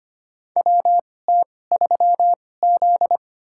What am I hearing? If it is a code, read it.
Morse code, heard as a keyed tone, about 25 wpm, spaced as for 18 wpm, 700 Hz. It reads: WT3Z
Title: USA Call Signs - 4 Characters - 2x Character Spacing - 25wpm